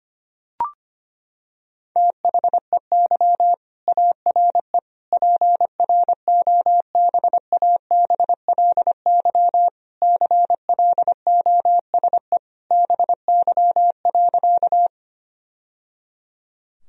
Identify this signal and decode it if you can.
Morse code, heard as a keyed tone, about 25 wpm, 700 Hz. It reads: THEY ARE PROBABLY CLOSE BY.
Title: Sentences from Top 1000 Words - Review Part 4 - 25wpm